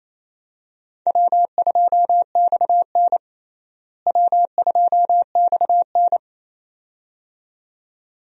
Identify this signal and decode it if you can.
Morse code, heard as a keyed tone, about 28 wpm, 700 Hz. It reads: W2XN W2XN
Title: USA Call Signs - 4 Characters - Repeated 2x - 28wpm